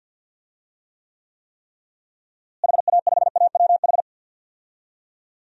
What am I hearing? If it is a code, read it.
Morse code, heard as a keyed tone, about 52 wpm, 700 Hz. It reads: HA5AWH
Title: DXCC-100 Call Signs - 6 Characters - Repeated 3x - 50wpm